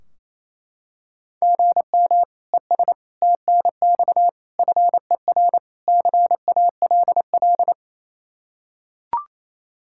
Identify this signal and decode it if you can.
Morse code, heard as a keyed tone, about 28 wpm, 700 Hz. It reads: GM ES TNX FER CALL
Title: K1USN SST QSO Practice 28wpm